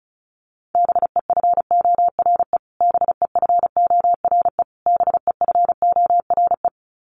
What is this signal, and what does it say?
Morse code, heard as a keyed tone, about 34 wpm, 700 Hz. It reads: BEFORE BEFORE BEFORE
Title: Top 200 Words Repeated 35wpm